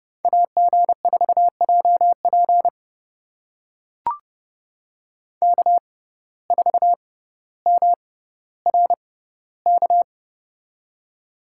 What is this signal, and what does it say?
Morse code, heard as a keyed tone, about 29 wpm, spaced as for 15 wpm, 700 Hz. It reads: AG4JP K4MRK
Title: USA Call Signs - 5 Characters - 6x Character Spacing - 30wpm